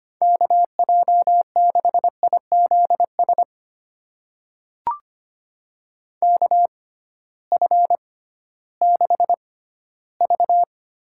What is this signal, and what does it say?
Morse code, heard as a keyed tone, about 25 wpm, spaced as for 14 wpm, 700 Hz. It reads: KJ6IZS KF6V
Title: USA Call Signs - 6 Characters - 6x Character Spacing - 25wpm